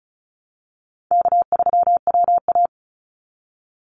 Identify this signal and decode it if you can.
Morse code, heard as a keyed tone, about 34 wpm, 700 Hz. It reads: K3WA
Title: Call Signs Repeated 3x - 35wpm